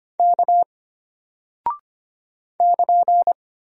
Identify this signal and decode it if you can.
Morse code, heard as a keyed tone, about 25 wpm, 700 Hz. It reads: K <KN>
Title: Single Character (Letter:Number:Punctuation:Prosign) Alt 25wpm